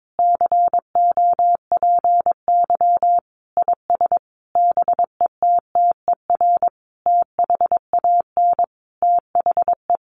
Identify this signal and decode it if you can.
Morse code, heard as a keyed tone, about 22 wpm, 700 Hz. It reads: COPY IS BETTER THAN THE
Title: Sentences from Top 1400 Words 22wpm